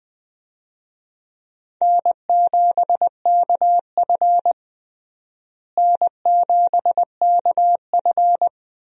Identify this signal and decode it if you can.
Morse code, heard as a keyed tone, about 20 wpm, 700 Hz. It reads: N7KF N7KF
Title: USA Call Signs - 4 Characters - Repeated 3x - 20wpm